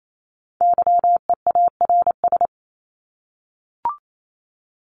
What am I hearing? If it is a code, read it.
Morse code, heard as a keyed tone, about 28 wpm, 700 Hz. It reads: YEARS
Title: ES:S Suffix - Top 100 Words with Base Words 28wpm